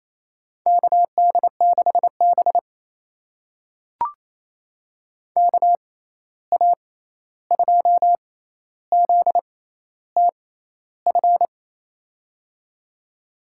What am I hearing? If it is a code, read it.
Morse code, heard as a keyed tone, about 28 wpm, spaced as for 13 wpm, 700 Hz. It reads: KD6B KA2ZTF
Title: USA Call Signs - 6x Character Spacing - 28wpm